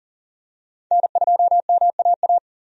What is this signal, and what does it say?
Morse code, heard as a keyed tone, about 37 wpm, 700 Hz. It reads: N2MAA
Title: USA Call Signs - 5 Characters - 40wpm